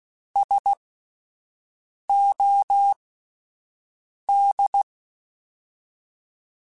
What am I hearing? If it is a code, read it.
Morse code, heard as a keyed tone, about 16 wpm, spaced as for 5 wpm, 800 Hz. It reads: SOD